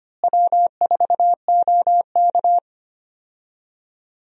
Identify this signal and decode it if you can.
Morse code, heard as a keyed tone, about 25 wpm, 700 Hz. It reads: W4OK
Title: USA Call Signs - 4 Characters - Repeated 3x - 25wpm